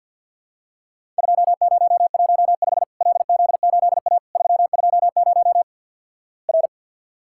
Morse code, heard as a keyed tone, about 50 wpm, 700 Hz. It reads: 2015 LZ8A 320 R
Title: Worked All Europe DX Contest - QTC Exchange - 50wpm